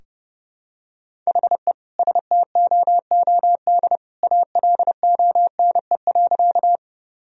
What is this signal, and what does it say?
Morse code, heard as a keyed tone, about 30 wpm, 700 Hz. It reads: HE STOOD ALONE.
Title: Sentences from Top 1000 Words - Review Part 2 - 30wpm